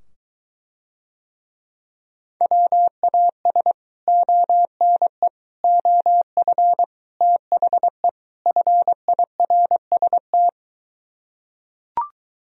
Morse code, heard as a keyed tone, about 23 wpm, 700 Hz. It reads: WAS ONE OF THE FIRST